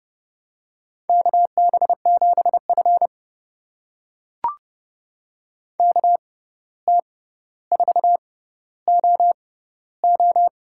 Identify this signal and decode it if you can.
Morse code, heard as a keyed tone, about 31 wpm, spaced as for 14 wpm, 700 Hz. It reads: KB7F KT4OO
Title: USA Call Signs - 6x Character Spacing - 30wpm